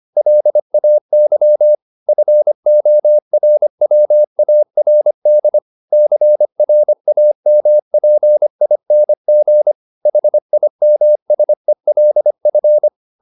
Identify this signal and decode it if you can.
Morse code, heard as a keyed tone, about 25 wpm, 600 Hz. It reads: LAY FORWARD CRAMPING HIMSELF